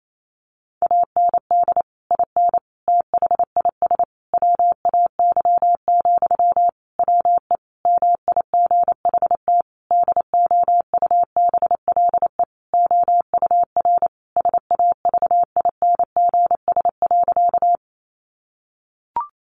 Morse code, heard as a keyed tone, about 28 wpm, 700 Hz. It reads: AND IN THIS WAY, WE MIGHT DOUBLE OUR SAVINGS.
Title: Sentences from Top 1400 Words 28wpm